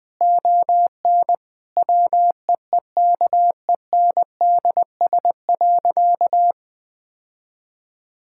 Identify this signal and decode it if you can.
Morse code, heard as a keyed tone, about 20 wpm, 700 Hz. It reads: ON WEEKENDS.
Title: Difficult Sentences with Hints - 20wpm